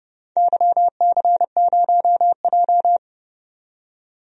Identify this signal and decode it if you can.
Morse code, heard as a keyed tone, about 30 wpm, 700 Hz. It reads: YC0J
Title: DXCC-100 Call Signs - 4 Characters - 30wpm